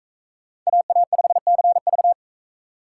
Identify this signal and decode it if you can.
Morse code, heard as a keyed tone, about 42 wpm, 700 Hz. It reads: AA5CV